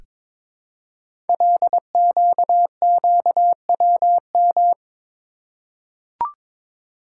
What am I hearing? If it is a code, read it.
Morse code, heard as a keyed tone, about 22 wpm, 700 Hz. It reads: LQQWM
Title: Code Groups - 5 Letters - 22wpm